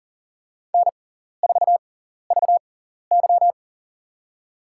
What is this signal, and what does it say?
Morse code, heard as a keyed tone, about 40 wpm, spaced as for 14 wpm, 700 Hz. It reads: N4VY
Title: USA Call Signs - 4 Characters - 6x Character Spacing - 40wpm